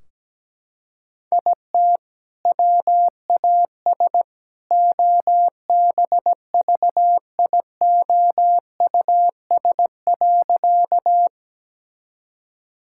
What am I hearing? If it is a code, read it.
Morse code, heard as a keyed tone, about 17 wpm, 700 Hz. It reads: IT WAS OBVIOUS.
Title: Sentences from Top 1600 Words 17wpm